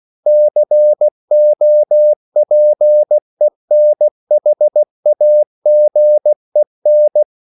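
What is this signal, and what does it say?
Morse code, heard as a keyed tone, about 16 wpm, 600 Hz. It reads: COPENHAGEN
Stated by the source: CW practice of national capitals at 16 words per minute, from ON6ZQ.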